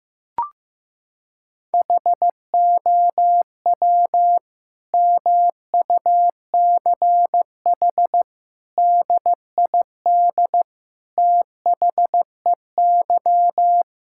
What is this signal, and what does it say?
Morse code, heard as a keyed tone, about 16 wpm, 700 Hz. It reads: HOW MUCH DID THEY
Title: Sentences from Top 1000 Words - Review Part 5 - 15wpm